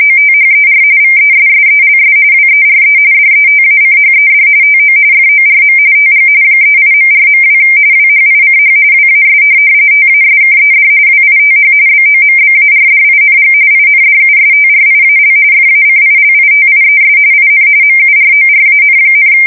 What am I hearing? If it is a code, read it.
RTTY, about 45.45 baud. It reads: OTHER VESSELS WERE DETACHED TO FORM TG 38.4.2 WHICH WAS TASKED WITH BOMBARDING INDUSTRIAL FACILITIES IN MURORAN,
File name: fsk170 / USS-Missouri